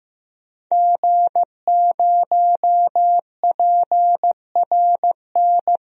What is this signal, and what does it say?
Morse code, heard as a keyed tone, about 15 wpm, 700 Hz. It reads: G0PRN
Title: DXCC-100 Call Signs - 5 Characters - 15wpm